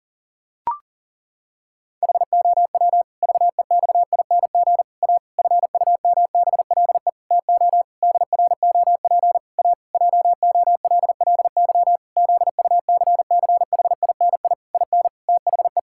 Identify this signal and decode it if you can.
Morse code, heard as a keyed tone, about 37 wpm, 700 Hz. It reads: HOW VEXING A FUMBLE TO DROP A JOLLY ZUCCHINI IN THE
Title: Pangram Challenge v2 - 40wpm